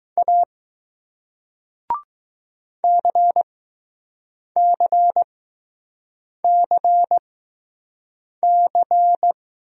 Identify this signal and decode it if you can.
Morse code, heard as a keyed tone, about 20 wpm, spaced as for 6 wpm, 700 Hz. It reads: A CCCC